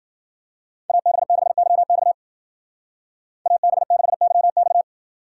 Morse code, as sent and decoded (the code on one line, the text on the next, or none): .- -... -.... -.-- -..- / .- -... -.... -.-- -..-
AB6YX AB6YX